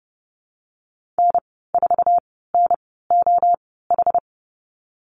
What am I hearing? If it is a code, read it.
Morse code, heard as a keyed tone, about 30 wpm, spaced as for 17 wpm, 700 Hz. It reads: N4NOH
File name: Lesson 37g - Letter x - Call Signs 30wpm